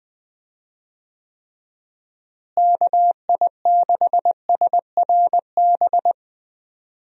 Morse code, heard as a keyed tone, about 20 wpm, 700 Hz. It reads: KI6SRB
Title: USA Call Signs - 6 Characters - Repeated 3x - 20wpm